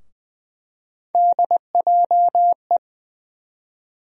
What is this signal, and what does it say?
Morse code, heard as a keyed tone, about 21 wpm, 700 Hz. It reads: DJE
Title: USA Call Sign Postfixes - 3 Letters 20wpm